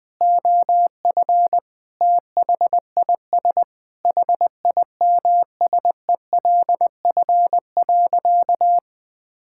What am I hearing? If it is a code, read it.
Morse code, heard as a keyed tone, about 20 wpm, 700 Hz. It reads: OF THIS HIMSELF.
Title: Sentences from Top 1600 Words 20wpm